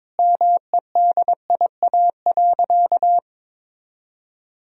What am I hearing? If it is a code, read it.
Morse code, heard as a keyed tone, about 22 wpm, 700 Hz. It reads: MEDIA.
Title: Sentences from Top 1000 Words - Review Part 4 - 22wpm